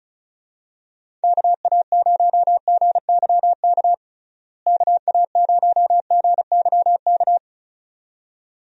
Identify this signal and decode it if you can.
Morse code, heard as a keyed tone, about 36 wpm, 700 Hz. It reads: KA0GYK KA0GYK
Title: USA Call Signs - 6 Characters - Repeated 2x - 35wpm